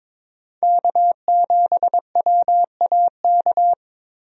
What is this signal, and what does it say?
Morse code, heard as a keyed tone, about 21 wpm, 700 Hz. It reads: K7WAK
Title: USA Call Signs - 5 Characters - 22wpm